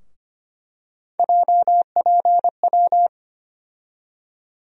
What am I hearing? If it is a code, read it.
Morse code, heard as a keyed tone, about 25 wpm, 700 Hz. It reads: JPW
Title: USA Call Sign Postfixes - 3 Letters 25wpm